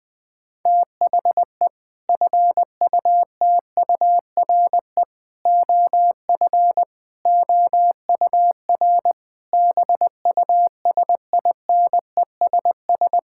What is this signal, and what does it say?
Morse code, heard as a keyed tone, about 21 wpm, 700 Hz. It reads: THE FUTURE OF OUR BUSINESS
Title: Sentences from Top 1000 Words - Review Part 6 - 20wpm